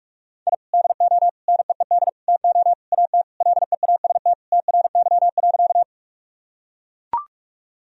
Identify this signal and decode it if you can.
Morse code, heard as a keyed tone, about 44 wpm, 700 Hz. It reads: I DO NEED TO AT LEAST TRY.
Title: Sentences from Top 1000 Words - Review Part 1 - 45wpm